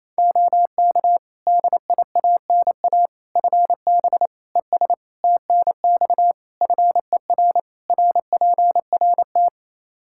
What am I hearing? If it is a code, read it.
Morse code, heard as a keyed tone, about 28 wpm, 700 Hz. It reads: OK DIANA FB ES TNX FER RPRT
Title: Practice QSO - Exchange 2 - Info - 28wpm